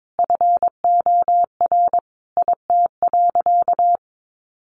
Morse code, heard as a keyed tone, about 22 wpm, 700 Hz. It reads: FOR IT.
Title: Sentences from Top 1000 Words 22wpm